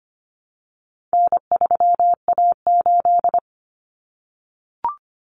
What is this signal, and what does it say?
Morse code, heard as a keyed tone, about 25 wpm, 700 Hz. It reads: N3A8
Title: Common Cut Numbers - 4-Digit 25wpm